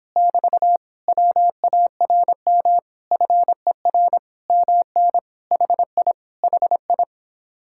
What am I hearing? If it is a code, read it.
Morse code, heard as a keyed tone, about 26 wpm, 700 Hz. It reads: <BT> WARM FER MN HI HI